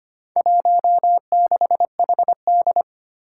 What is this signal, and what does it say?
Morse code, heard as a keyed tone, about 25 wpm, 700 Hz. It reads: 16HD
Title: Four Challenging Letter-Number Combinations 25wpm